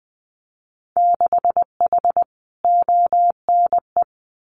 Morse code, heard as a keyed tone, about 20 wpm, 700 Hz. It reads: 6H ONE
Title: Winter Field Day Exchanges Encore 20wpm